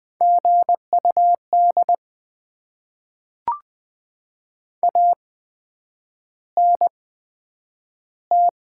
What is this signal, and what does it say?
Morse code, heard as a keyed tone, about 20 wpm, spaced as for 9 wpm, 700 Hz. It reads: GUD ANT
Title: Top 20 QSO Elements - 8x Character Spacing - 20wpm